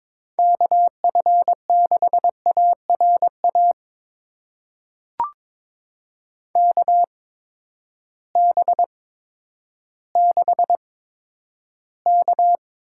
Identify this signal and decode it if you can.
Morse code, heard as a keyed tone, about 22 wpm, spaced as for 11 wpm, 700 Hz. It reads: KF6ARA KB6K
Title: USA Call Signs - 6 Characters - 8x Character Spacing - 22wpm